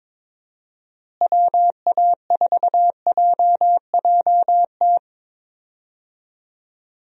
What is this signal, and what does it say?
Morse code, heard as a keyed tone, about 22 wpm, 700 Hz. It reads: WA4JJT
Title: USA Call Signs - 6 Characters - Repeated 2x - 22wpm